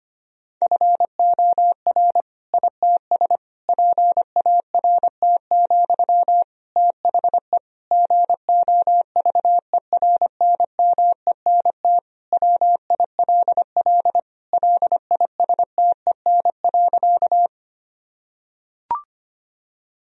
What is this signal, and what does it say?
Morse code, heard as a keyed tone, about 25 wpm, 700 Hz. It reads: FOR ITS PART, THE GOVERNMENT WILL LISTEN.
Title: Sentences from Top 1000 Words - Review Part 3 - 25wpm